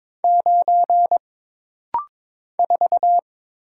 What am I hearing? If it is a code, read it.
Morse code, heard as a keyed tone, about 22 wpm, 700 Hz. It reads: 9 4
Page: This rapid-fire timing is geared to push your ICR (Instant Character Recognition) skills.